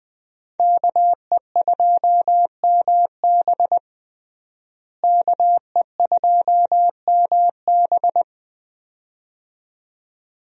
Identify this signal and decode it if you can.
Morse code, heard as a keyed tone, about 20 wpm, 700 Hz. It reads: KE2MB KE2MB
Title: USA Call Signs - 5 Characters - Repeated 2x - 20wpm